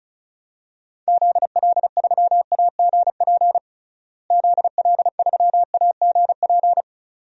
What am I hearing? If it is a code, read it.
Morse code, heard as a keyed tone, about 34 wpm, 700 Hz. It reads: ZL3AGP ZL3AGP
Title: DXCC-100 Call Signs - 6 Characters - Repeated 2x - 35wpm